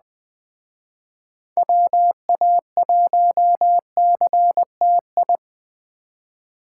Morse code, heard as a keyed tone, about 20 wpm, 700 Hz. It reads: WA1CTI
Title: Speed Racing - USA Call Signs 17wpm